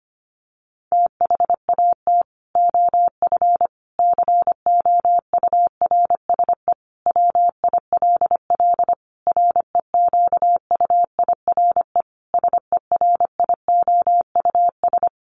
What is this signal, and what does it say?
Morse code, heard as a keyed tone, about 25 wpm, 700 Hz. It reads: THAT OF COURSE WILL REQUIRE SERIOUS
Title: Sentences from Top 1500 Words 25wpm